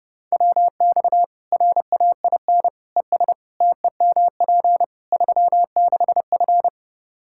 Morse code, heard as a keyed tone, about 32 wpm, 700 Hz. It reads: WX RAIN ES TEMP 36F
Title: Practice QSO - Exchange 2 - Info - 30wpm